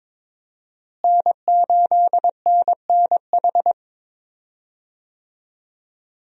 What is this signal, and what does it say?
Morse code, heard as a keyed tone, about 22 wpm, 700 Hz. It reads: N8NNH
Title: USA Call Signs - 5 Characters - Repeated 2x - 22wpm